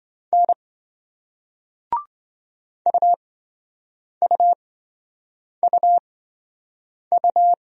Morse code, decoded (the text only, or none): N UUUU